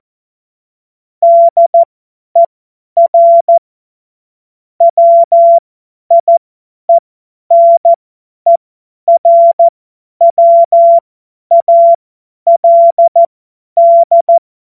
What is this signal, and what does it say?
Morse code, heard as a keyed tone, about 14 wpm, spaced as for 10 wpm, 680 Hz. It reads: DER WIENERWALD
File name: Morsekurs / Witze / Witz 20. Juni 2023 50 F70